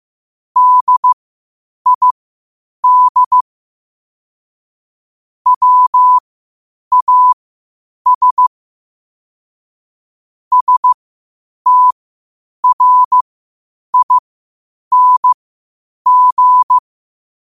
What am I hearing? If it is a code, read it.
Morse code, heard as a keyed tone, about 15 wpm, spaced as for 8 wpm, 1000 Hz. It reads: DID WAS STRING
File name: qotdfiles / 8wpm